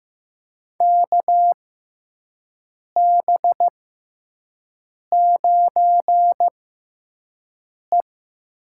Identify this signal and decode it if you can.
Morse code, heard as a keyed tone, about 16 wpm, spaced as for 5 wpm, 700 Hz. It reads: KB9E
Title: USA Call Signs - 6 Characters - 6x Character Spacing - 15wpm